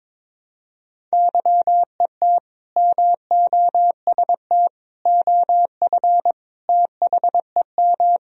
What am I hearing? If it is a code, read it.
Morse code, heard as a keyed tone, about 21 wpm, 700 Hz. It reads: YET MOST OF THEM
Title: Sentences from Top 1000 Words - Review Part 1 - 22wpm